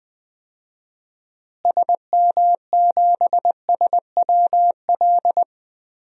Morse code, heard as a keyed tone, about 21 wpm, 700 Hz. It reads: SM7SWL
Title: DXCC-100 Call Signs - 6 Characters - Repeated 2x - 20wpm